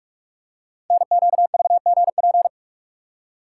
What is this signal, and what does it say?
Morse code, heard as a keyed tone, about 44 wpm, 690 Hz. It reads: NQVGP
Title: Code Groups - 5 Letters - 45wpm